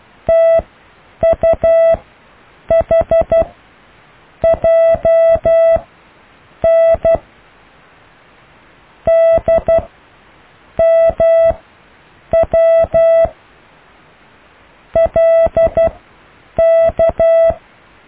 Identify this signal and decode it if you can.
Morse code, heard as a keyed tone, about 12 wpm, spaced as for 7 wpm, 660 Hz. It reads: TUHJN DMWLK
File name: cw / IJLUM+CDRO+EKTPV+BFHNW